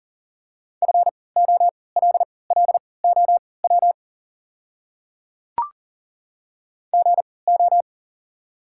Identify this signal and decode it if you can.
Morse code, heard as a keyed tone, about 40 wpm, spaced as for 23 wpm, 690 Hz. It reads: FOLLOW GO
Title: 3x Character Spacing - Top 200 Words - 40wpm